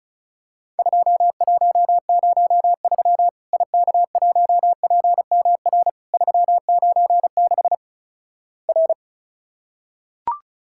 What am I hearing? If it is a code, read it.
Morse code, heard as a keyed tone, about 36 wpm, 700 Hz. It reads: 2103 IK1PMR 396 R
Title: Worked All Europe DX Contest - QTC Exchange - 35wpm